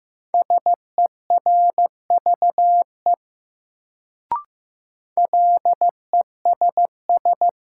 Morse code, heard as a keyed tone, about 15 wpm, 700 Hz. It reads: SERVE LESS
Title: Sentences from Top 400 Words - New Word Drill - 15wpm